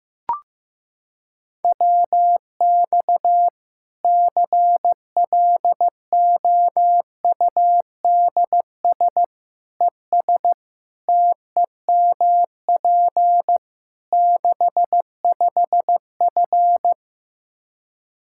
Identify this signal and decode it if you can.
Morse code, heard as a keyed tone, about 16 wpm, 700 Hz. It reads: WX CLOUDS ES TEMP 65F
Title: Practice QSO - Exchange 2 - Info - 15wpm